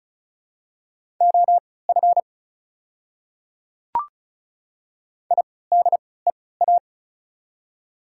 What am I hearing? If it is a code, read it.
Morse code, heard as a keyed tone, about 36 wpm, spaced as for 20 wpm, 700 Hz. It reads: OF IDEA
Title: 3x Character Spacing - Top 300 Words - 35wpm